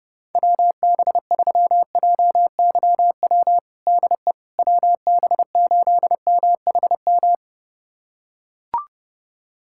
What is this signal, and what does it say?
Morse code, heard as a keyed tone, about 30 wpm, 700 Hz. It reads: WB3JYW DE WB8MHM
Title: USA Callsign DE Callsign - 6 Characters - 30wpm